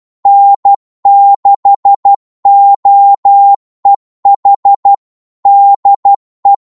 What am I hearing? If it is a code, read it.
Morse code, heard as a keyed tone, about 12 wpm, 800 Hz. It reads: N6OEH DE
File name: Morse / qso-12wpm